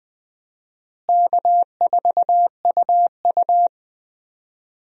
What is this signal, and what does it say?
Morse code, heard as a keyed tone, about 21 wpm, 700 Hz. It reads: K4UU
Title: USA Call Signs - 4 Characters - Repeated 2x - 20wpm